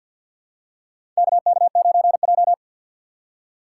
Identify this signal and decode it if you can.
Morse code, heard as a keyed tone, about 48 wpm, 700 Hz. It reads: KK9J
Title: USA Call Signs - 4 Characters - 2x Character Spacing - 50wpm